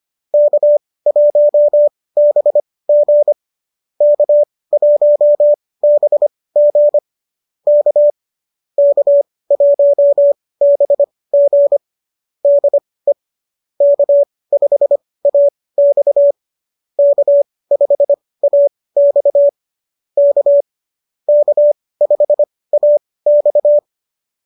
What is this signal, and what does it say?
Morse code, heard as a keyed tone, about 25 wpm, spaced as for 18 wpm, 590 Hz. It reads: K1BG K1BG K K1BG DE K5AX K5AX K K5AX